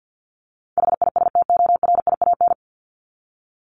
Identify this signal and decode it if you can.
Morse code, heard as a keyed tone, about 50 wpm, 700 Hz. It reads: HISTORIAN
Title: Top 1751-2000 Words 50wpm